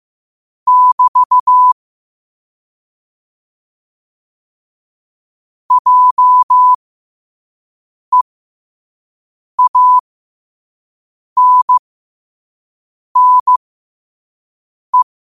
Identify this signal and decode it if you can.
Morse code, heard as a keyed tone, about 15 wpm, spaced as for 5 wpm, 1000 Hz. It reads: <BT> JEANNE
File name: qotdfiles / 5wpm